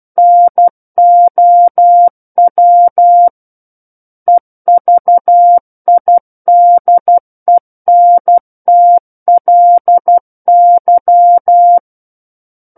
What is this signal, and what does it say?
Morse code, heard as a keyed tone, about 12 wpm, 700 Hz. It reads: NOW EVIDENTLY
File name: Book One / 03-Chapter 3 - 12 WPM